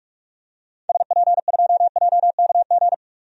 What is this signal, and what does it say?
Morse code, heard as a keyed tone, about 44 wpm, 700 Hz. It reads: SP2JKG
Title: DXCC-100 Call Signs - 6 Characters - 45wpm